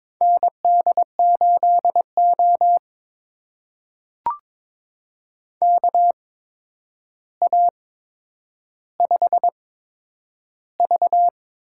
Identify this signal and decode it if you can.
Morse code, heard as a keyed tone, about 22 wpm, spaced as for 9 wpm, 700 Hz. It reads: ND8O KA5V
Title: USA Call Signs - 4 Characters - 8x Character Spacing - 22wpm